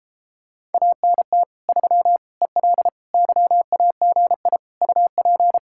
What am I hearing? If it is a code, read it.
Morse code, heard as a keyed tone, about 31 wpm, 700 Hz. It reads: ANT 3 EL YAGI UP